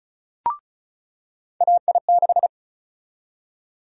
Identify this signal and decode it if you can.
Morse code, heard as a keyed tone, about 34 wpm, 700 Hz. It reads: AI6
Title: USA Call Sign Prefixes - 35wpm